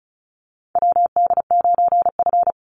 Morse code, heard as a keyed tone, about 34 wpm, 700 Hz. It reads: WD9F
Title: NA SOTA Chaser Callsigns 35wpm